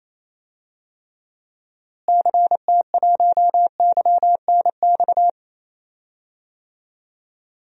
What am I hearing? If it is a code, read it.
Morse code, heard as a keyed tone, about 28 wpm, 700 Hz. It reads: CT1YNX
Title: DXCC-100 Call Signs - 6 Characters - Repeated 3x - 28wpm